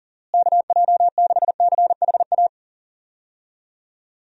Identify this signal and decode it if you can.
Morse code, heard as a keyed tone, about 39 wpm, 700 Hz. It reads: KJ6CHA
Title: USA Call Signs - 6 Characters - Repeated 2x - 40wpm